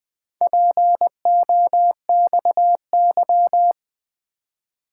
Morse code, heard as a keyed tone, about 20 wpm, 700 Hz. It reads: POXY
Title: Four Letter-Number Combinations - 20wpm